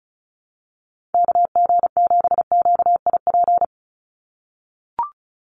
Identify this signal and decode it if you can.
Morse code, heard as a keyed tone, about 34 wpm, 700 Hz. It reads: KG7QIP
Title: USA Call Signs 35wpm